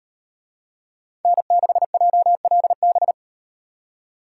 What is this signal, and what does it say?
Morse code, heard as a keyed tone, about 38 wpm, 700 Hz. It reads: N6JLB